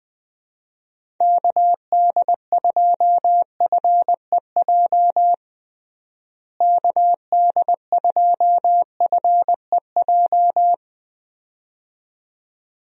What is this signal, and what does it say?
Morse code, heard as a keyed tone, about 21 wpm, 700 Hz. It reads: KD2FEJ KD2FEJ
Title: USA Call Signs - 6 Characters - Repeated 2x - 20wpm